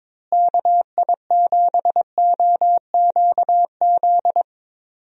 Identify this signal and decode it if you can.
Morse code, heard as a keyed tone, about 22 wpm, 700 Hz. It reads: KI7OQZ
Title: USA Call Signs - 6 Characters - Repeated 3x - 22wpm